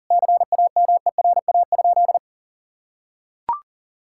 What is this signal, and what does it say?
Morse code, heard as a keyed tone, about 40 wpm, 700 Hz. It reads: CAMERA?
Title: Sentences from Top 1000 Words - Review Part 5 - 40wpm